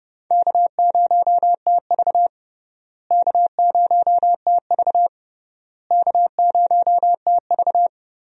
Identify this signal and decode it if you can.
Morse code, heard as a keyed tone, about 30 wpm, 700 Hz. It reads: K0TV K0TV K0TV
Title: USA Call Signs - 4 Characters - Repeated 3x - 30wpm